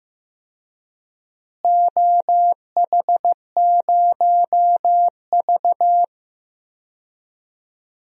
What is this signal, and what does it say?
Morse code, heard as a keyed tone, about 15 wpm, 700 Hz. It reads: OH0V
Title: DXCC-100 Call Signs - 4 Characters - Repeated 2x - 15wpm